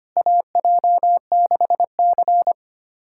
Morse code, heard as a keyed tone, about 25 wpm, 700 Hz. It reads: AJ6C
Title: USA Callsign DE Callsign - 4 Characters - 25wpm